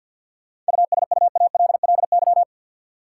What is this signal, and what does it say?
Morse code, heard as a keyed tone, about 50 wpm, 700 Hz. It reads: USUALLY
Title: Sentences from Top 1400 Words - New Word Drill Alt - 50wpm